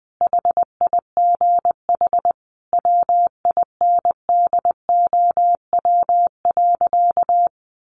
Morse code, heard as a keyed tone, about 20 wpm, 700 Hz. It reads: HIGH WINDOW.
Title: A Sound Of Thunder 20wpm